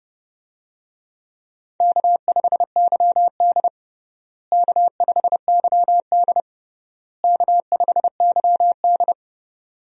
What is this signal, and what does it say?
Morse code, heard as a keyed tone, about 30 wpm, 700 Hz. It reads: K5YD K5YD K5YD
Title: USA Call Signs - 4 Characters - Repeated 3x - 30wpm